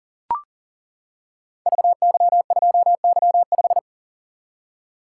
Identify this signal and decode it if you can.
Morse code, heard as a keyed tone, about 41 wpm, 700 Hz. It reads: VY2Y5
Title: Code Groups - 5 Letter:Numbers - 40wpm